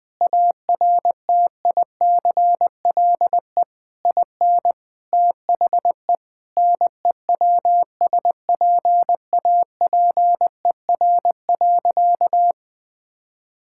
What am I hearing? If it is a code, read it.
Morse code, heard as a keyed tone, about 21 wpm, 700 Hz. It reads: ARTICLE IN THE NEWSPAPER.
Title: Sentences from Top 1000 Words - Review Part 5 - 20wpm